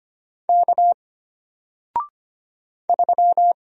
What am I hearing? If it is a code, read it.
Morse code, heard as a keyed tone, about 25 wpm, 700 Hz. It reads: K 3
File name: Single-Letter-Number-25wpm